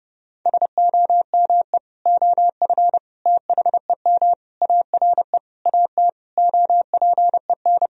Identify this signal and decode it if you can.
Morse code, heard as a keyed tone, about 32 wpm, 700 Hz. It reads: SOME OF THEM ARE AT OPEN
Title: Sentences from Top 1000 Words - Review Part 2 - 30wpm